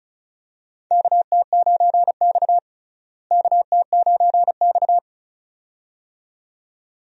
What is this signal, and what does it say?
Morse code, heard as a keyed tone, about 34 wpm, 700 Hz. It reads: KT9X KT9X
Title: USA Call Signs - 4 Characters - Repeated 2x - 35wpm